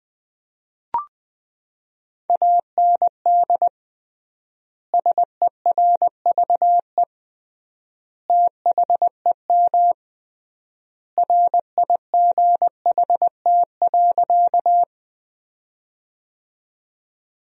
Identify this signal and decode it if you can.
Morse code, heard as a keyed tone, about 21 wpm, 700 Hz. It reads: AND SERVE THEM RIGHT.
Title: 3x Word-Spacing - Sentences from Top 400 Words - 20wpm